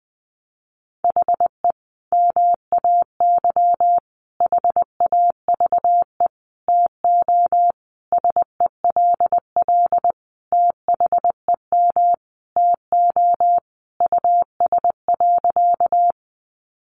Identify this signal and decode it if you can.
Morse code, heard as a keyed tone, about 20 wpm, 700 Hz. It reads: HE MAY HAVE TO SELL THEM TO US.
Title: Sentences from Top 1100 Words 20wpm